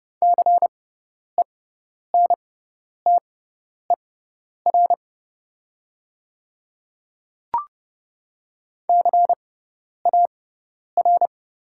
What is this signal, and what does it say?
Morse code, heard as a keyed tone, about 30 wpm, spaced as for 10 wpm, 690 Hz. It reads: CENTER CAR
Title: 6x Character Spacing - Top 500 Words - 30wpm